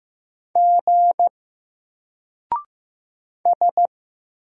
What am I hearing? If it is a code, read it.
Morse code, heard as a keyed tone, about 15 wpm, 700 Hz. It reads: G S
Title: Single Letter-Number Alt 15wpm